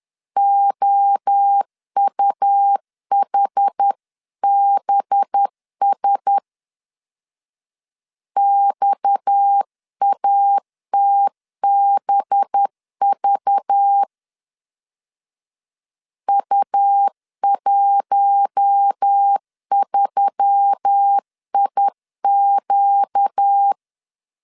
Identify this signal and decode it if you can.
Morse code, heard as a keyed tone, about 10 wpm, 800 Hz. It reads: OUHBS XATBV U13IQ
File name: CW / DET / 1 6 1500C S B 10wpm 4137